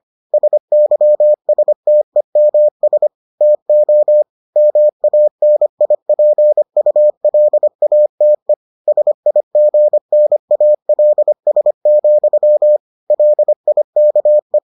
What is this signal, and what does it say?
Morse code, heard as a keyed tone, about 25 wpm, 600 Hz. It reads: SYSTEMS TO MANIPULATE SIGNALS, LIKE